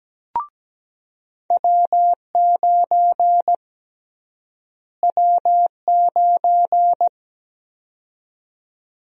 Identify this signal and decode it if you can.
Morse code, heard as a keyed tone, about 17 wpm, 700 Hz. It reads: W9 W9
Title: USA Call Sign Prefixes - Repeated 2x - 17wpm